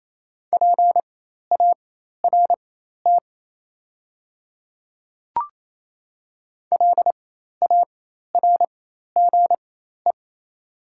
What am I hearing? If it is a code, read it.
Morse code, heard as a keyed tone, about 28 wpm, spaced as for 13 wpm, 690 Hz. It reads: PART LARGE
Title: 4x Character Spacing - Top 300 Words - 28wpm